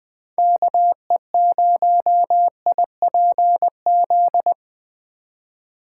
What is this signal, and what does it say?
Morse code, heard as a keyed tone, about 20 wpm, 700 Hz. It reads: KE0IPZ